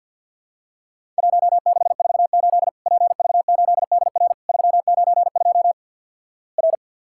Morse code, heard as a keyed tone, about 50 wpm, 700 Hz. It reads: 1648 PV8DR 392 R
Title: Worked All Europe DX Contest - QTC Exchange - 50wpm